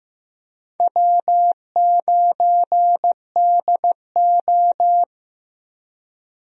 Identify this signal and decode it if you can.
Morse code, heard as a keyed tone, about 15 wpm, 700 Hz. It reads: W9DO
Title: USA Call Signs - 4 Characters - Repeated 3x - 15wpm